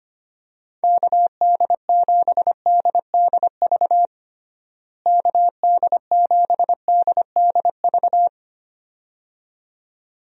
Morse code, heard as a keyed tone, about 25 wpm, 700 Hz. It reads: KD7DDV KD7DDV
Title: USA Call Signs - 6 Characters - Repeated 2x - 25wpm